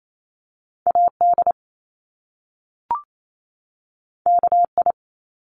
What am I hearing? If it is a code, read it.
Morse code, heard as a keyed tone, about 28 wpm, 700 Hz. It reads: AD KI
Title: Top Two Letter Combinations in Words 28wpm